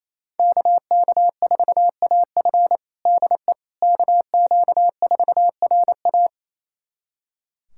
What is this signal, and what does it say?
Morse code, heard as a keyed tone, about 29 wpm, 700 Hz. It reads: KK4AF DE KQ4RA
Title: USA Callsign DE Callsign - 5 Characters - 28wpm